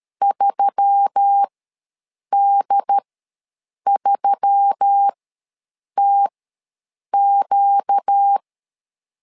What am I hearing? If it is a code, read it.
Morse code, heard as a keyed tone, about 13 wpm, spaced as for 7 wpm, 800 Hz. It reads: 3D3TQ
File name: LEZIONI / 4 Q1S3 ws 12wpm 2038